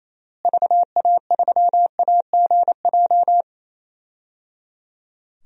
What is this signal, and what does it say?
Morse code, heard as a keyed tone, about 28 wpm, 700 Hz. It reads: VA3AGJ
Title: DXCC-100 Call Signs - 6 Characters - Repeated 3x - 28wpm